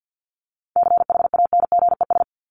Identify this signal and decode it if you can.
Morse code, heard as a keyed tone, about 48 wpm, 700 Hz. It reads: CHANGES
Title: ES:S Suffix - Top 100 Words with Base Words 50wpm